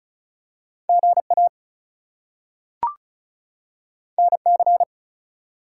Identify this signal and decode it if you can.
Morse code, heard as a keyed tone, about 36 wpm, 700 Hz. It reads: GA NC
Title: US States and Canadian Provinces 35wpm